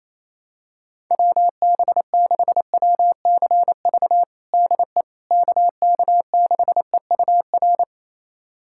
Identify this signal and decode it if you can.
Morse code, heard as a keyed tone, about 27 wpm, 700 Hz. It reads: WB6WCV DE KK6EUR
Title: USA Callsign DE Callsign - 6 Characters - 28wpm